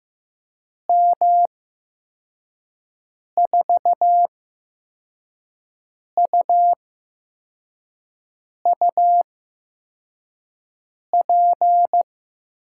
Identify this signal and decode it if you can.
Morse code, heard as a keyed tone, about 15 wpm, spaced as for 4 wpm, 700 Hz. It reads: M4UUP
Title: USA Call Signs - 8x Character Spacing - 15wpm